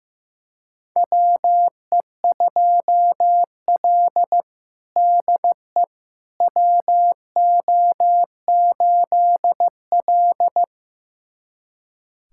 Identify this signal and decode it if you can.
Morse code, heard as a keyed tone, about 15 wpm, 700 Hz. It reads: WE2L DE WO8L
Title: USA Callsign DE Callsign - 4 Characters - 15wpm